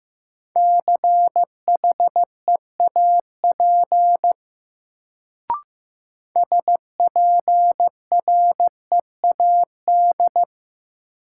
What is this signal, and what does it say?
Morse code, heard as a keyed tone, about 15 wpm, 700 Hz. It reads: CHEAP SPREAD
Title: Sentences from Top 1200 Words - New Word Drill Alt - 15wpm